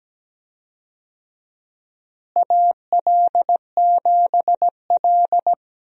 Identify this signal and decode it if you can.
Morse code, heard as a keyed tone, about 17 wpm, 700 Hz. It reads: AL7L
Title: USA Call Signs - 4 Characters - Repeated 3x - 17wpm